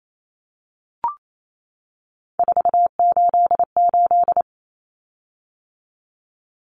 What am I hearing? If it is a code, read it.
Morse code, heard as a keyed tone, about 28 wpm, 700 Hz. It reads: 488
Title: Common Cut Numbers - 3-Digit 28wpm